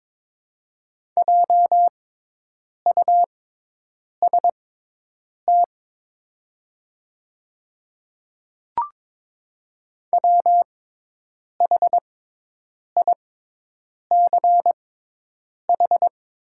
Morse code, heard as a keyed tone, about 22 wpm, spaced as for 8 wpm, 700 Hz. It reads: JUST WHICH
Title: 6x Character Spacing - Top 200 Words - 22wpm